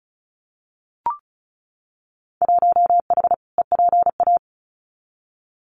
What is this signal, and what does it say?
Morse code, heard as a keyed tone, about 34 wpm, 700 Hz. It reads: 1H EPA
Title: Winter Field Day Exchanges 35wpm